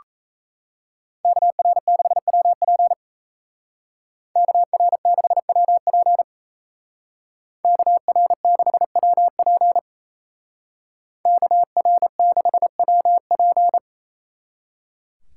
Morse code, heard as a keyed tone, about 35 wpm, 700 Hz. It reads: KR6WP KR6WP KR6WP KR6WP
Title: Speed Racing - USA 2x2 Call Signs 28wpm